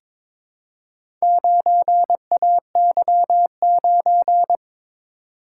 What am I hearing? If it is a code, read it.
Morse code, heard as a keyed tone, about 22 wpm, 700 Hz. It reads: 9AY9
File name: practice / 22wpm / Four Letter-Number Combinations - 22wpm